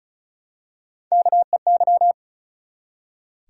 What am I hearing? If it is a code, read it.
Morse code, heard as a keyed tone, about 34 wpm, 700 Hz. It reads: KEY
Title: Three Letter Words - 35wpm